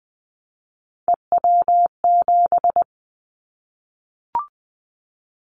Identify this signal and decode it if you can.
Morse code, heard as a keyed tone, about 20 wpm, 700 Hz. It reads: EW7
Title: Callsign Prefixes for Top 50 DXCC - 20wpm